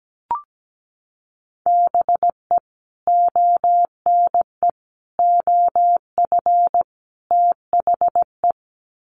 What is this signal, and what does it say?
Morse code, heard as a keyed tone, about 17 wpm, 700 Hz. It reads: BE ONE OF THE
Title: Sets of 4 Words 17wpm